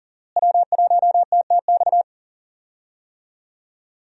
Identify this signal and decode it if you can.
Morse code, heard as a keyed tone, about 38 wpm, 700 Hz. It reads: W1TTX
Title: USA Call Signs - 5 Characters - Repeated 2x - 40wpm